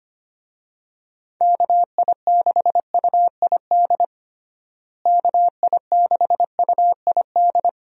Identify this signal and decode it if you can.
Morse code, heard as a keyed tone, about 25 wpm, 700 Hz. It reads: KI6UID KI6UID
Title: USA Call Signs - 6 Characters - Repeated 2x - 25wpm